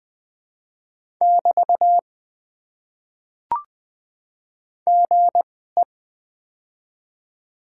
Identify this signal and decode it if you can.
Morse code, heard as a keyed tone, about 20 wpm, spaced as for 14 wpm, 700 Hz. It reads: <BT> GE
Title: Top 90 QSO Elements - 2x Character Spacing - 20wpm